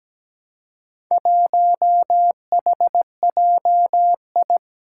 Morse code, heard as a keyed tone, about 17 wpm, 700 Hz. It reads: 1HJI
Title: Four Challenging Letter-Number Combinations 17wpm